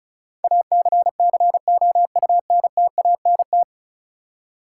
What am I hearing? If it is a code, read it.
Morse code, heard as a keyed tone, about 36 wpm, 700 Hz. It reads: ACCOUNTANT
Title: Practice QSO - Exchange 3 - More Info - 35wpm